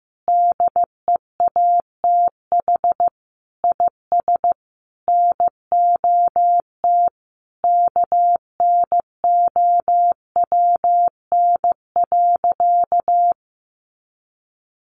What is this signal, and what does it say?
Morse code, heard as a keyed tone, about 15 wpm, 700 Hz. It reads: DEATH IS NOT KNOWN.
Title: Sentences from Top 1100 Words 15wpm